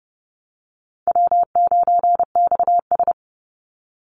Morse code, heard as a keyed tone, about 30 wpm, 700 Hz. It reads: W9XS
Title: Easy Call Signs - 30wpm